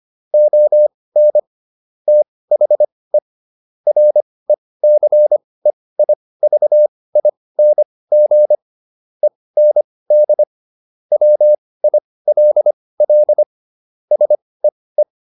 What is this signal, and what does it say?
Morse code, heard as a keyed tone, about 25 wpm, spaced as for 18 wpm, 600 Hz. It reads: ON THE RECEIVING END WILL SEE